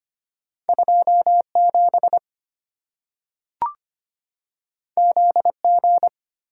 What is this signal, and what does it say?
Morse code, heard as a keyed tone, about 25 wpm, 700 Hz. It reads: 27 ZG
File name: Two Letter-Number Combinations - 25wpm